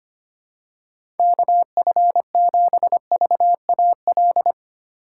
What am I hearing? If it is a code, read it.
Morse code, heard as a keyed tone, about 25 wpm, 700 Hz. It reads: KF7VAL
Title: USA Call Signs - 6 Characters - 2x Character Spacing - 25wpm